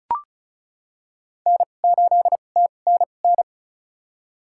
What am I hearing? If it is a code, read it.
Morse code, heard as a keyed tone, about 36 wpm, spaced as for 25 wpm, 700 Hz. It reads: N8TNN
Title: USA Call Signs - 5 Characters - 2x Character Spacing - 35wpm